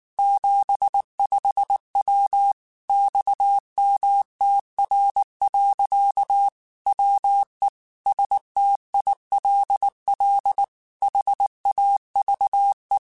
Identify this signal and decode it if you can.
Morse code, heard as a keyed tone, about 19 wpm, 800 Hz. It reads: 75W XMTR. WE STILL HAVE
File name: QSOs / QSOs-mp3s / WA7QOX 1@18